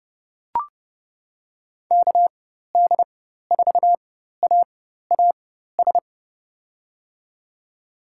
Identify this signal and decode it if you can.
Morse code, heard as a keyed tone, about 32 wpm, spaced as for 14 wpm, 700 Hz. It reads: KD4AAS
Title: USA Call Signs - 4x Character Spacing - 30wpm